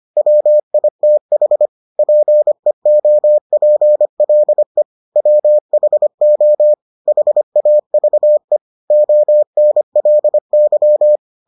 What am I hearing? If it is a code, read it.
Morse code, heard as a keyed tone, about 25 wpm, 600 Hz. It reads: WITH PEOPLE WHO HAVE ONLY